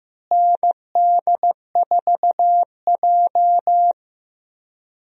Morse code, heard as a keyed tone, about 15 wpm, 700 Hz. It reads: ND4J
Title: USA Call Signs - 4 Characters - Repeated 3x - 15wpm